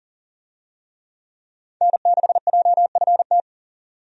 Morse code, heard as a keyed tone, about 40 wpm, 700 Hz. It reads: N6JFT
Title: USA Call Signs - 5 Characters - 40wpm